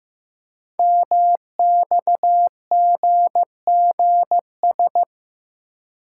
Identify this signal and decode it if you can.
Morse code, heard as a keyed tone, about 15 wpm, 700 Hz. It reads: MXGGS
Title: Code Groups - 5 Letters - 15wpm